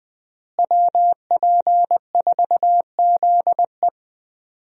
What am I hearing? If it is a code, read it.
Morse code, heard as a keyed tone, about 20 wpm, 700 Hz. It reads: WP4ZE
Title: USA Call Signs - 5 Characters - Repeated 3x - 20wpm